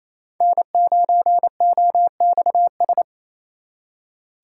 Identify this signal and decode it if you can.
Morse code, heard as a keyed tone, about 33 wpm, 700 Hz. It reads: N9OXS
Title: Speed Racing - USA 1x3 Call Signs 28wpm